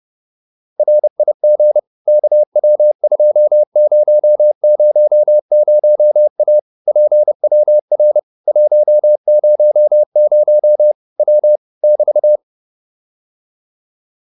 Morse code, heard as a keyed tone, about 30 wpm, 600 Hz. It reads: RIG KW2000A PWR 100 W <BT>